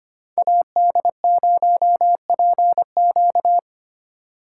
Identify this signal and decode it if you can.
Morse code, heard as a keyed tone, about 31 wpm, 700 Hz. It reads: AD0PQ
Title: Speed Racing - USA Call Signs 25wpm